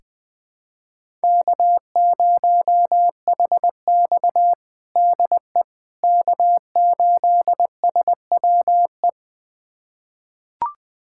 Morse code, heard as a keyed tone, about 20 wpm, 700 Hz. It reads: K0HX DE K8SWE
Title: USA Callsign DE Callsign 20wpm